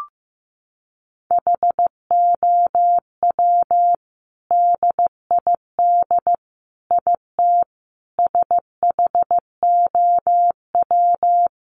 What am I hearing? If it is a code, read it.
Morse code, heard as a keyed tone, about 15 wpm, 700 Hz. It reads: HOW DID IT SHOW
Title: Sentences from Top 800 Words 15wpm